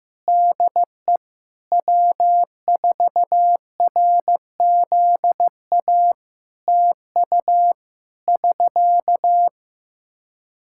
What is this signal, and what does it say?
Morse code, heard as a keyed tone, about 15 wpm, 700 Hz. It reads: DE W4RZA TU <SK>
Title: Practice QSO - Exchange 4 - Ending - 15wpm